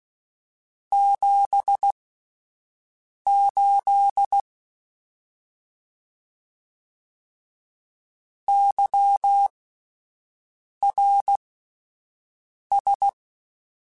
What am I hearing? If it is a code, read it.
Morse code, heard as a keyed tone, about 16 wpm, spaced as for 5 wpm, 800 Hz. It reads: 78 YRS
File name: (spaced) / QSOs Spaced-mp3s / WA7QOX 2 @10 sp